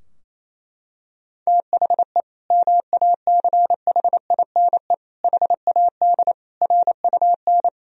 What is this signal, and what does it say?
Morse code, heard as a keyed tone, about 28 wpm, 700 Hz. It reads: THE MACHINE HAD RUN
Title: Sentences from Top 1000 Words - Review Part 2 - 28wpm